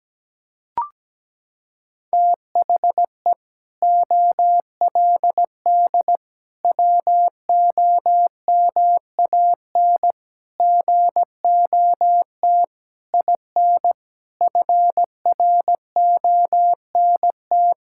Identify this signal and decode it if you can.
Morse code, heard as a keyed tone, about 17 wpm, 700 Hz. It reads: THE OLD WOMAN GOT IN FRONT
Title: Sentences from Top 1600 Words 17wpm